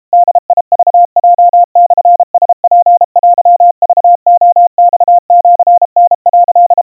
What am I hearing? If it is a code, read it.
Morse code (Japanese Wabun code), heard as a keyed tone, about 32 wpm, 700 Hz. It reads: ボクヲモラツテクレマシタ」
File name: kawa / Morse sample / kamyu 120 cpm 700 Hz Crispy